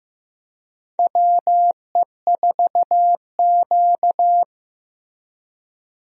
Morse code, heard as a keyed tone, about 15 wpm, 700 Hz. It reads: WE4Q
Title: USA Call Signs - 4 Characters - Repeated 3x - 15wpm